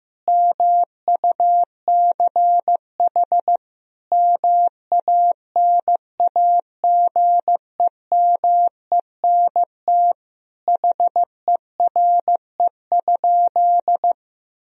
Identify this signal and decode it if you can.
Morse code, heard as a keyed tone, about 15 wpm, 700 Hz. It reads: MUCH MANAGEMENT HERE?
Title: Sentences from Top 1000 Words - Review Part 4 - 15wpm